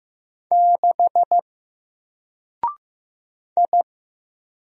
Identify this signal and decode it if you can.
Morse code, heard as a keyed tone, about 15 wpm, 700 Hz. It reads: 6 I
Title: Single Character (Letter:Number:Punctuation:Prosign) Alt 15wpm